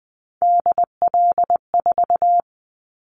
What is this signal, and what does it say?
Morse code, heard as a keyed tone, about 20 wpm, 700 Hz. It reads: DL4
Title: Callsign Prefixes for Top 25 DXCC - 20wpm